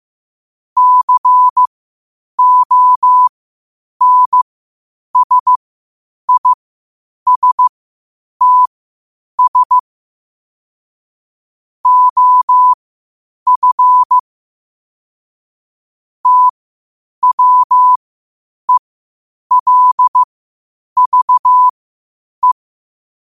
Morse code, heard as a keyed tone, about 15 wpm, spaced as for 8 wpm, 1000 Hz. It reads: CONSISTS OF TWELVE
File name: qotdfiles / 8wpm